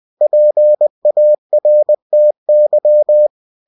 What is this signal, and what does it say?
Morse code, heard as a keyed tone, about 20 wpm, 600 Hz. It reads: PARTY
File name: cw-books / winnie-the-pooh / winnie-20wpm-page-0005